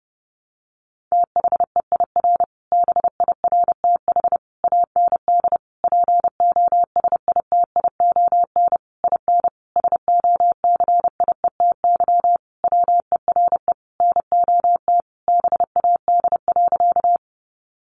Morse code, heard as a keyed tone, about 30 wpm, 700 Hz. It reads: THEIR BIRTH AND POSITION IN SOCIETY WERE NOT BAD.
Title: Sentences from Top 900 Words 30wpm